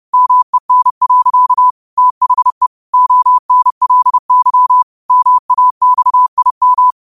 Morse code, heard as a keyed tone, about 30 wpm, 1000 Hz. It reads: MEN. THE ONLY MAXIM
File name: qotdfiles / 30wpm